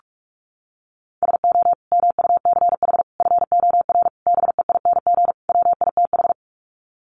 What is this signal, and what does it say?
Morse code, heard as a keyed tone, about 44 wpm, 700 Hz. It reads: SO MUCH FOR BEING WITH
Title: Sets of 5 Words Encore 45wpm